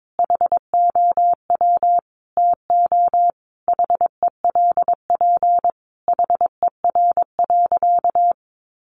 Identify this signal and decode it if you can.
Morse code, heard as a keyed tone, about 22 wpm, 700 Hz. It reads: HOW TO HELP HER.
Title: Sentences from Top 1000 Words 22wpm